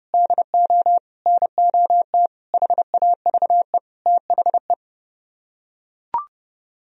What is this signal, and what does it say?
Morse code, heard as a keyed tone, about 31 wpm, 700 Hz. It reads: DO NOT HAVE THE